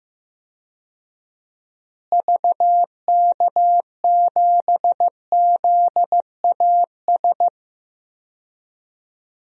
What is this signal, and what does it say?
Morse code, heard as a keyed tone, about 15 wpm, 700 Hz. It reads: VK7ZAS
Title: DXCC-100 Call Signs - 6 Characters - Repeated 2x - 15wpm